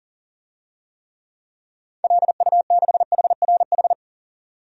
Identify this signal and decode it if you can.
Morse code, heard as a keyed tone, about 40 wpm, 700 Hz. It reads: LU6HRH
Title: DXCC-100 Call Signs - 6 Characters - 40wpm